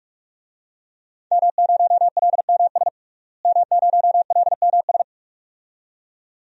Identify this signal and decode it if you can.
Morse code, heard as a keyed tone, about 47 wpm, 700 Hz. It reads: M0LMS M0LMS
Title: DXCC-100 Call Signs - 5 Characters - Repeated 2x - 45wpm